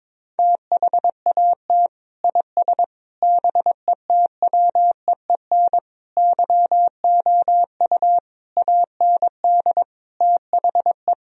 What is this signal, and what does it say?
Morse code, heard as a keyed tone, about 21 wpm, 700 Hz. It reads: THAT IS BETWEEN YOU AND THE
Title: Sentences from Top 1000 Words - Review Part 4 - 22wpm